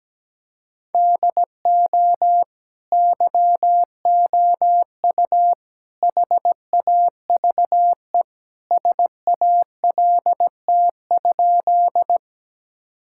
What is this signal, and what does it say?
Morse code, heard as a keyed tone, about 17 wpm, 700 Hz. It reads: DO YOU HAVE SALT?
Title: Sentences from Top 1000 Words - Review Part 5 - 17wpm